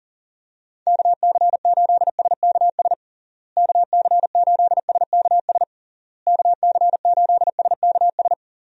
Morse code, heard as a keyed tone, about 40 wpm, 700 Hz. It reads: KC8SKS KC8SKS KC8SKS
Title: USA Call Signs - 6 Characters - Repeated 3x - 40wpm